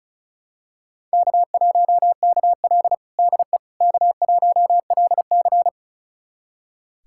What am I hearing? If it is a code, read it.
Morse code, heard as a keyed tone, about 34 wpm, 700 Hz. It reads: K1KL DE K1LC
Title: USA Callsign DE Callsign - 4 Characters - 35wpm